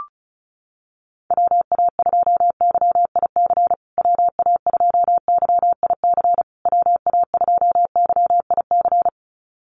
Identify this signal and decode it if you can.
Morse code, heard as a keyed tone, about 34 wpm, 700 Hz. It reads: WA2YIC WA2YIC WA2YIC
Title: USA Call Signs Repeated 3x 35wpm